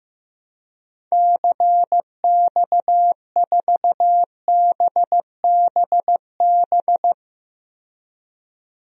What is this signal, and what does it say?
Morse code, heard as a keyed tone, about 15 wpm, 700 Hz. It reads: CX4BBB
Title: DXCC-100 Call Signs - 6 Characters - 15wpm